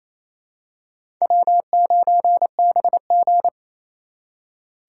Morse code, heard as a keyed tone, about 28 wpm, 700 Hz. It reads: W9BG
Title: USA Call Signs - 4 Characters - 3x Character Spacing - 28wpm